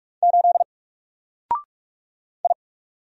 Morse code, heard as a keyed tone, about 43 wpm, 700 Hz. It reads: 8 I
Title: Single Letter-Number Alt 45wpm